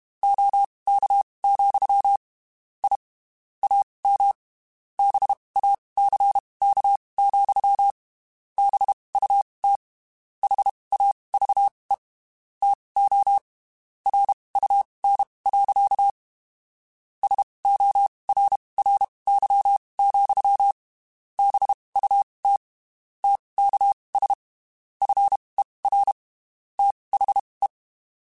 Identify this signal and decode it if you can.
Morse code, heard as a keyed tone, about 31 wpm, spaced as for 23 wpm, 800 Hz. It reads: OK, I AM BACK, BUT HAVE TO RUN. SORRY, BUT TKS FER THE
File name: (spaced) / QSOs Spaced-mp3s / kc7sac @30 sp